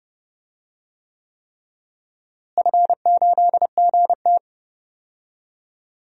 Morse code, heard as a keyed tone, about 29 wpm, 700 Hz. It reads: F8GT
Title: DXCC-100 Call Signs - 4 Characters - Repeated 3x - 30wpm